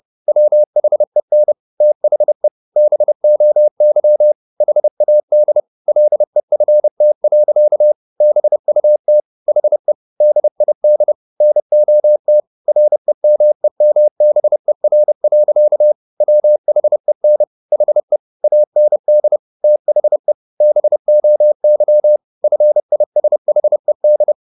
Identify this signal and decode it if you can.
Morse code, heard as a keyed tone, about 30 wpm, 600 Hz. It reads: WHEN THE BOY HAD LEFT. BUT HE DID NOT REMEMBER. WHEN HE AND THE BOY FISHED
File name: the old man and the sea / the old man and the sea 1 30wpm